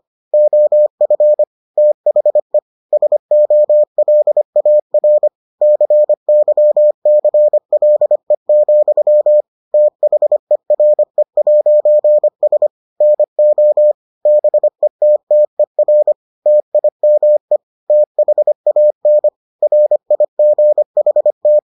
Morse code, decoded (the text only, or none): OF THE SOLAR CYCLE, THERE'S NO BETTER TIME THAN RIGHT